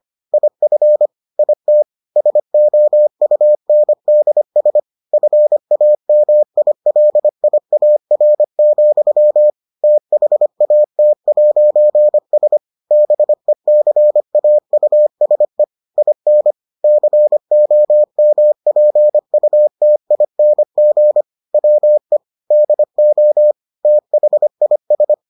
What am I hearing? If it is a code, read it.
Morse code, heard as a keyed tone, about 25 wpm, 600 Hz. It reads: IF IT SOUNDS FAMILIAR, THAT'S BECAUSE IN COMPUTING WE DO THIS